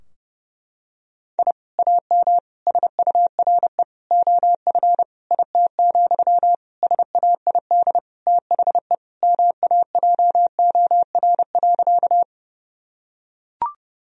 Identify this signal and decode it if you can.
Morse code, heard as a keyed tone, about 30 wpm, 700 Hz. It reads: I AM SURE OF IT, SAID THE MAJOR.
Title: Sentences from Top 1000 Words - Review Part 4 - 30wpm